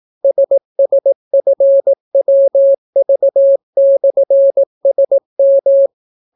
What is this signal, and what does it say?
Morse code, heard as a keyed tone, about 18 wpm, 550 Hz. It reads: SSFWV/SM